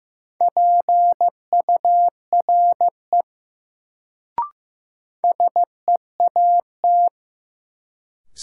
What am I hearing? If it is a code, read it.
Morse code, heard as a keyed tone, about 16 wpm, 700 Hz. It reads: PURE SEAT
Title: Sentences from Top 1500 Words - New Word Drill - 15wpm